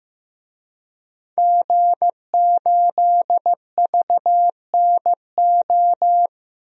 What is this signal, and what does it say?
Morse code, heard as a keyed tone, about 15 wpm, 700 Hz. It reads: G8VNO
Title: DXCC-100 Call Signs - 5 Characters - 15wpm